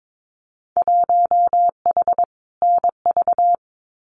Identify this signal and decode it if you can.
Morse code, heard as a keyed tone, about 22 wpm, 700 Hz. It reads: 1H NV
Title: Winter Field Day Exchanges 22wpm